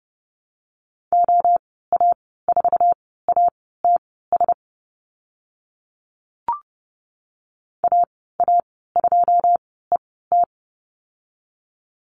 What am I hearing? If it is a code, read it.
Morse code, heard as a keyed tone, about 30 wpm, spaced as for 17 wpm, 700 Hz. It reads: OA4ATS AA2ET
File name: course / 30:10wpm / 088 Lesson 15g - Number 5 - Call Signs 30wpm